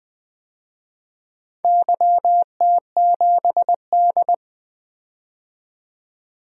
Morse code, heard as a keyed tone, about 20 wpm, 700 Hz. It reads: YT7D
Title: DXCC-100 Call Signs - 4 Characters - 20wpm